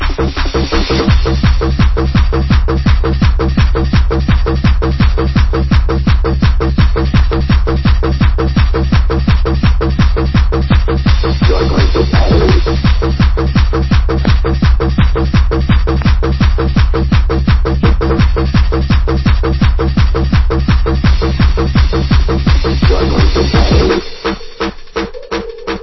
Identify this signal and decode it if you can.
There is no code.